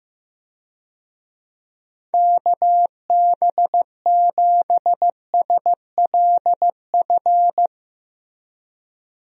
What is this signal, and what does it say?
Morse code, heard as a keyed tone, about 14 wpm, 700 Hz. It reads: KB7SLF
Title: USA Call Signs - 6 Characters - Repeated 2x - 15wpm